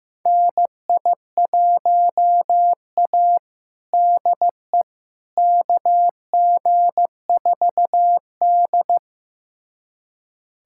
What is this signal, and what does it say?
Morse code, heard as a keyed tone, about 16 wpm, 700 Hz. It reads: NI1A DE KG4D
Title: USA Callsign DE Callsign - 4 Characters - 15wpm